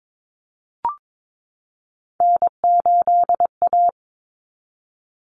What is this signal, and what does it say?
Morse code, heard as a keyed tone, about 22 wpm, 700 Hz. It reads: N8A
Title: Common Cut Numbers - 3-Digit 22wpm